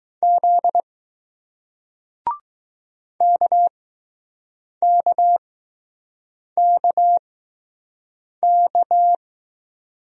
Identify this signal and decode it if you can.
Morse code, heard as a keyed tone, about 20 wpm, spaced as for 6 wpm, 700 Hz. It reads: Z KKKK